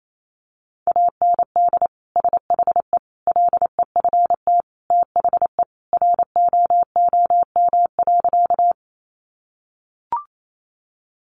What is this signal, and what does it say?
Morse code, heard as a keyed tone, about 28 wpm, 700 Hz. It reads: AND SHE LEFT THE ROOM.
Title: Sentences from Top 300 Words 28wpm